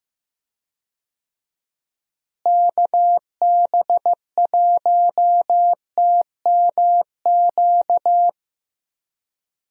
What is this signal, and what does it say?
Morse code, heard as a keyed tone, about 15 wpm, 700 Hz. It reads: KB1TMQ
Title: USA Call Signs - 6 Characters - Repeated 2x - 15wpm